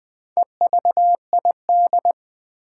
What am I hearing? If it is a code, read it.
Morse code, heard as a keyed tone, about 20 wpm, 700 Hz. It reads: EVID
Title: Build A Word - 2 Letters At A Time - 8-Letter Words - 20wpm